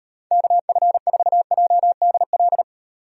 Morse code, heard as a keyed tone, about 38 wpm, 700 Hz. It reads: KF4JDL